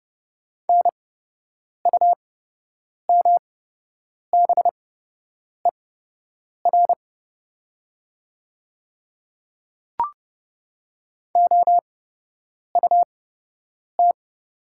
Morse code, heard as a keyed tone, about 30 wpm, spaced as for 8 wpm, 700 Hz. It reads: NUMBER OUT
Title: 8x Character Spacing - Top 100 Words - 30wpm